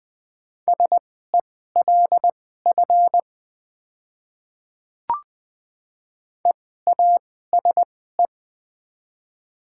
Morse code, heard as a keyed tone, about 20 wpm, spaced as for 14 wpm, 700 Hz. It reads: SELF EASE
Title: 2x Character Spacing - Top 300 Words - 20wpm